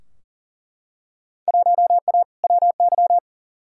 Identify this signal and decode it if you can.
Morse code, heard as a keyed tone, about 37 wpm, 700 Hz. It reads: 1A WY
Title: Field Day Exchanges Encore - 40wpm